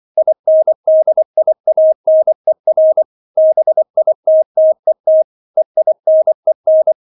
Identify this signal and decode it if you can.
Morse code, heard as a keyed tone, about 24 wpm, 630 Hz. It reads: INDIANER BITTET EINEN